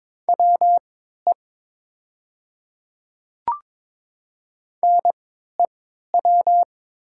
Morse code, heard as a keyed tone, about 22 wpm, spaced as for 12 wpm, 700 Hz. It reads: WE NEW
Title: 3x Character Spacing - Top 200 Words - 22wpm